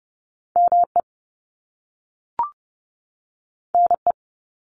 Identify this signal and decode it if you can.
Morse code, heard as a keyed tone, about 30 wpm, 700 Hz. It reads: ME NE
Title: US States 30wpm